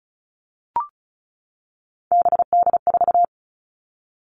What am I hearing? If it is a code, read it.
Morse code, heard as a keyed tone, about 34 wpm, 700 Hz. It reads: BD4
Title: All Cut Numbers - 3-Digit 35wpm